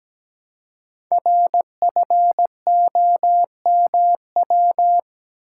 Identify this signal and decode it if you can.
Morse code, heard as a keyed tone, about 17 wpm, 700 Hz. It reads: RFOMW
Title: Code Groups - 5 Letters - 17wpm